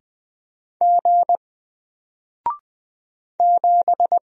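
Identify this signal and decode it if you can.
Morse code, heard as a keyed tone, about 20 wpm, 700 Hz. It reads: G 7
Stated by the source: Single letter and number Morse drill at 20 wpm, from N4LCD.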